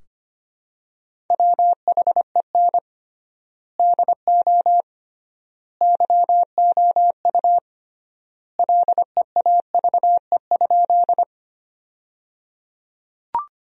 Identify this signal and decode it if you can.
Morse code, heard as a keyed tone, about 25 wpm, 700 Hz. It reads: WHEN DO YOU LEAVE?
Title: 3x Word-Spacing - Sentences from Top 400 Words - 25wpm